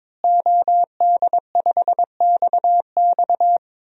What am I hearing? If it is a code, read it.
Morse code, heard as a keyed tone, about 21 wpm, 700 Hz. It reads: OD5XX
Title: DXCC-100 Call Signs - 5 Characters - Repeated 2x - 22wpm